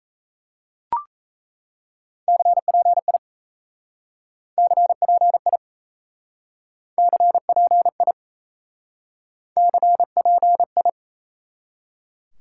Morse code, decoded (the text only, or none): CPI CPI CPI CPI